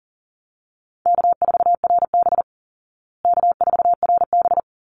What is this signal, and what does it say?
Morse code, heard as a keyed tone, about 40 wpm, 700 Hz. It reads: K4RB K4RB
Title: Easy Call Signs Repeated 2x - 40wpm